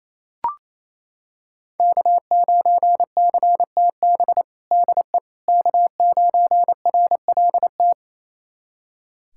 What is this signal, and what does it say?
Morse code, heard as a keyed tone, about 28 wpm, 700 Hz. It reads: K9CTB DE K9RLT
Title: USA Callsign DE Callsign - 5 Characters - 28wpm